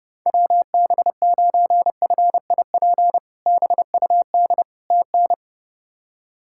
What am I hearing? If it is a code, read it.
Morse code, heard as a keyed tone, about 32 wpm, 700 Hz. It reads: WB9FIP BUD TN
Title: K1USN SST QSO Practice - 30wpm